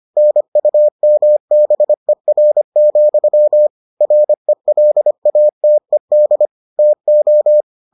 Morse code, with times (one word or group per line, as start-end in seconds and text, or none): 0.2-3.7 NUMBER,
4.0-6.5 RELATED
6.8-7.6 TO